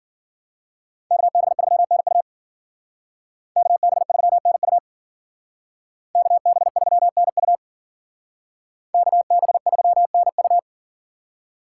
Morse code, heard as a keyed tone, about 52 wpm, 700 Hz. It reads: KB3NU KB3NU KB3NU KB3NU